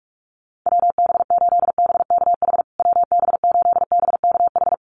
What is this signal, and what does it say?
Morse code, heard as a keyed tone, about 44 wpm, 700 Hz. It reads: WB8BKH WB8BKH
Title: USA Call Signs Repeated 3x 45wpm